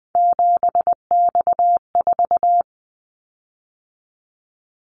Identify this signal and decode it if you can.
Morse code, heard as a keyed tone, about 20 wpm, 700 Hz. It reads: 7X4
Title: Callsign Prefixes for Top 100 DXCC - 20wpm